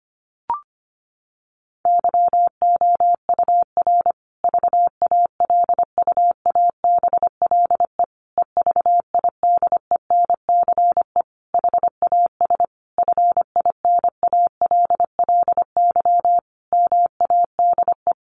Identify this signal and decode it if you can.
Morse code, heard as a keyed tone, about 25 wpm, 700 Hz. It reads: YOUR VALUABLE EVIDENCE HAS FINALLY MADE
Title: Sentences from Top 1300 Words 25wpm